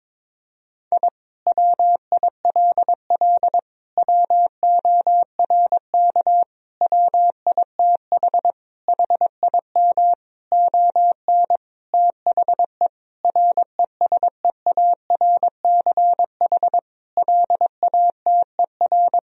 Morse code, heard as a keyed tone, about 22 wpm, 700 Hz. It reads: I WILL WORK WITH HIM ON THE RESEARCH LATER
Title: Sentences from Top 1000 Words - Review Part 4 - 22wpm